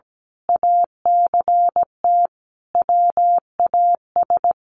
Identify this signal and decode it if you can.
Morse code, heard as a keyed tone, about 17 wpm, 700 Hz. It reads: ACT WAS
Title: Sentences from Top 200 Words 17wpm